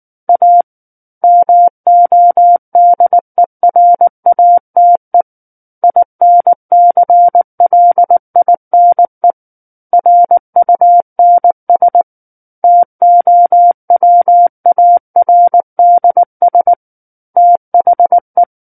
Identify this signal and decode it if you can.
Morse code, heard as a keyed tone, about 19 wpm, 700 Hz. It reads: A MODERATE INCLINE RUNS TOWARDS THE
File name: Book One / 10-Chapter 10 - 19 WPM